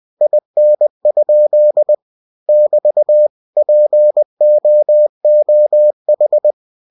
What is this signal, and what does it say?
Morse code, heard as a keyed tone, about 20 wpm, 600 Hz. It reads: IN? <BT> POOH
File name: winnie-20wpm